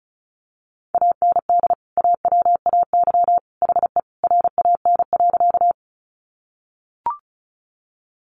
Morse code, heard as a keyed tone, about 34 wpm, 700 Hz. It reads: AND AWAY HE RAN.
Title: Sentences from Top 600 Words 35wpm